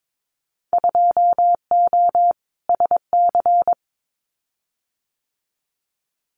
Morse code, heard as a keyed tone, about 22 wpm, 700 Hz. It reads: 2O SC
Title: Winter Field Day Exchanges Encore 22wpm